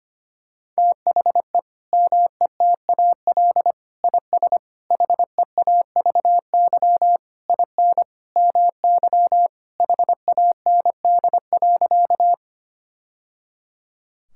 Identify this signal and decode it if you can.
Morse code, heard as a keyed tone, about 25 wpm, 700 Hz. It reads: THE METAL IS HEAVY IN MY HAND.
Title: Sentences from Top 1000 Words - Review Part 6 - 25wpm